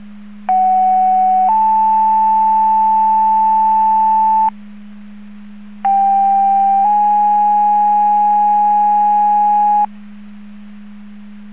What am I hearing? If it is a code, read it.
Two-tone page, heard as a one-second tone, then a three-second tone, about 750 then 880 Hz, about 790 then 830 Hz. It is heard from 0.5 to 9.8 s.